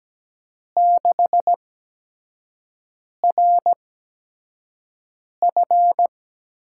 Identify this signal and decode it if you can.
Morse code, heard as a keyed tone, about 17 wpm, spaced as for 5 wpm, 700 Hz. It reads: 6RF
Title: USA Call Signs - 5 Characters - 8x Character Spacing - 17wpm